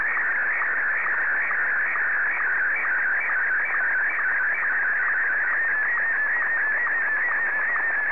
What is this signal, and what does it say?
SSTV (slow-scan television), Martin 1.
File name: sstv / M1-ZL2TJM